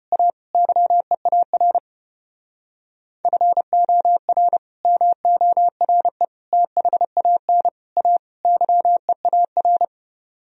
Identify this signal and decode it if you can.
Morse code, heard as a keyed tone, about 33 wpm, 700 Hz. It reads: A YEAR FOR MORE THAN A YEAR